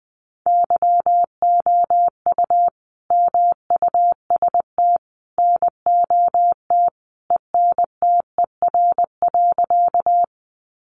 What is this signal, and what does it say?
Morse code, heard as a keyed tone, about 20 wpm, 700 Hz. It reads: YOU MUST NOT ENTER.
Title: Sentences from Top 1300 Words 20wpm